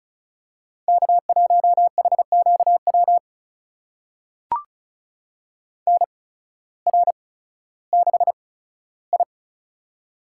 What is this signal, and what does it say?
Morse code, heard as a keyed tone, about 36 wpm, spaced as for 16 wpm, 700 Hz. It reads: K1HQW NR6I
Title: USA Call Signs - 8x Character Spacing - 35wpm